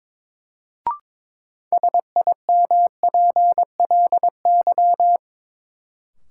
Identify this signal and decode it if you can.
Morse code, heard as a keyed tone, about 22 wpm, 700 Hz. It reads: SIMPLY
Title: Sentences from Top 600 Words - New Word Drill - 22wpm